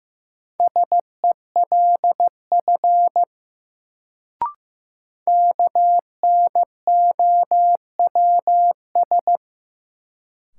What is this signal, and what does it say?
Morse code, heard as a keyed tone, about 15 wpm, 700 Hz. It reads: SELF KNOWS
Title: Sentences from Top 1000 Words - Review Part 1 - New Word Drill - 15wpm